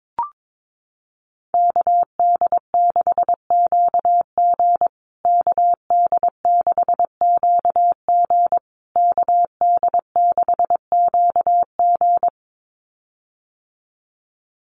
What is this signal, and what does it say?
Morse code, heard as a keyed tone, about 22 wpm, 700 Hz. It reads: KD6QG KD6QG KD6QG
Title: USA Call Signs Repeated 3x 22wpm